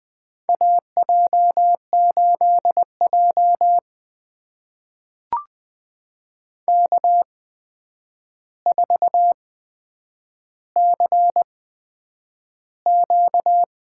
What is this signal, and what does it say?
Morse code, heard as a keyed tone, about 20 wpm, spaced as for 9 wpm, 700 Hz. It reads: AJ8J K4CQ
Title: USA Call Signs - 4 Characters - 8x Character Spacing - 20wpm